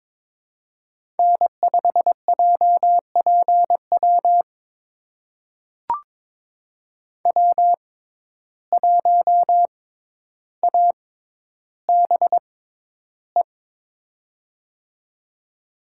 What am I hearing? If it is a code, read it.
Morse code, heard as a keyed tone, about 22 wpm, spaced as for 11 wpm, 700 Hz. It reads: N5JPW W1ABE
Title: USA Call Signs - 5 Characters - 6x Character Spacing - 22wpm